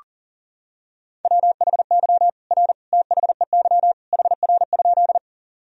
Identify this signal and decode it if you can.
Morse code, heard as a keyed tone, about 40 wpm, 700 Hz. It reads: WHY R THEY HR?
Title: Sentences from Top 200 Words and Abbreviations 40wpm